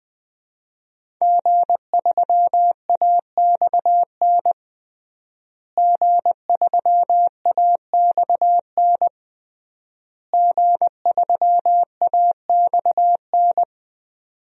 Morse code, heard as a keyed tone, about 20 wpm, 700 Hz. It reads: G3AXN G3AXN G3AXN
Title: DXCC-100 Call Signs - 5 Characters - Repeated 3x - 20wpm